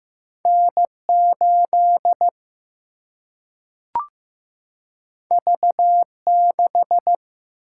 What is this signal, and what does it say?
Morse code, heard as a keyed tone, about 15 wpm, 700 Hz. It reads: N 8 V 6
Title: Single-Letter Single-Number 15wpm